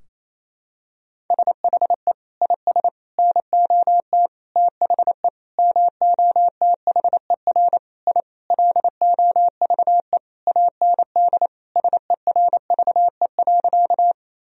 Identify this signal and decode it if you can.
Morse code, heard as a keyed tone, about 28 wpm, 700 Hz. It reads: SHE IS NOT THE MOTHER I LOVE AND SERVE.
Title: Sentences from Top 1000 Words - Review Part 2 - 28wpm